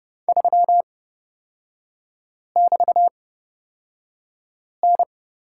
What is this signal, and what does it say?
Morse code, heard as a keyed tone, about 30 wpm, spaced as for 5 wpm, 700 Hz. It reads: 3<BT>N
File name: Lesson 36e - BT - Three Characters 30wpm